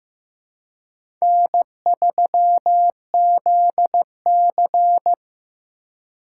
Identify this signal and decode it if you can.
Morse code, heard as a keyed tone, about 15 wpm, 700 Hz. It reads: N3ZC
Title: USA Call Signs - 4 Characters - 8x Character Spacing - 15wpm